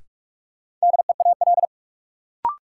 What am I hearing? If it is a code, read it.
Morse code, heard as a keyed tone, about 44 wpm, 700 Hz. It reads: DEAL
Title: Sets of 3 Words - New Word Drill - 45wpm